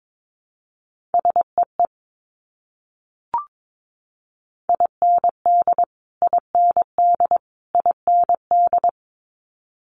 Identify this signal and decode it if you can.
Morse code, heard as a keyed tone, about 22 wpm, 700 Hz. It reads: SEE IND IND IND
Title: Multi-letter Phonemes Repeated 3x 22wpm